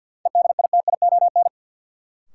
Morse code, heard as a keyed tone, about 50 wpm, 700 Hz. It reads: EDITION
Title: Sets of 5 Words - New Word Drill - 50wpm